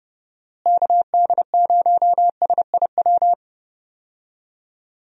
Morse code, heard as a keyed tone, about 30 wpm, 700 Hz. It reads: KD0SIW
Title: USA Call Signs - 6 Characters - Repeated 2x - 30wpm